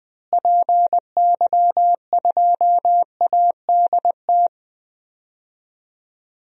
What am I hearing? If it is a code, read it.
Morse code, heard as a keyed tone, about 21 wpm, 700 Hz. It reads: PY2ADT
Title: DXCC-100 Call Signs - 6 Characters - Repeated 2x - 20wpm